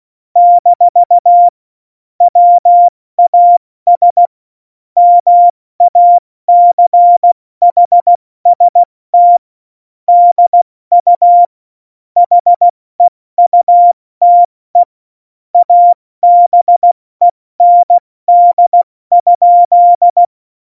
Morse code, heard as a keyed tone, about 16 wpm, 700 Hz. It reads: - WAS MACHST DU HEUTE ABEND?
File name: Morsekurs / Witze / Witz 21. November 2017, 70 F80